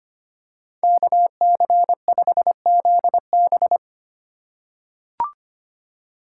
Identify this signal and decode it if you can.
Morse code, heard as a keyed tone, about 25 wpm, 700 Hz. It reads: KC5ZB
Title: USA Call Signs - 5 Characters - 2x Character Spacing - 25wpm